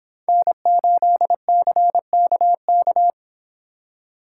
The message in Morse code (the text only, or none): N8CKK